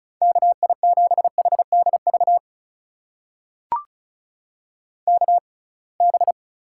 Morse code, heard as a keyed tone, about 36 wpm, spaced as for 27 wpm, 700 Hz. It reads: KI7HDV KB
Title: USA Call Signs - 6x Character Spacing - 35wpm